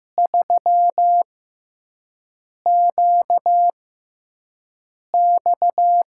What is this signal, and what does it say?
Morse code, heard as a keyed tone, about 15 wpm, spaced as for 5 wpm, 700 Hz. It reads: 3QX
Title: USA Call Signs - 6x Character Spacing - 15wpm